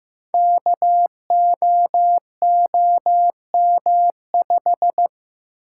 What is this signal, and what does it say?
Morse code, heard as a keyed tone, about 15 wpm, 700 Hz. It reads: KOOM5
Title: Code Groups - 5 Letter:Numbers - 15wpm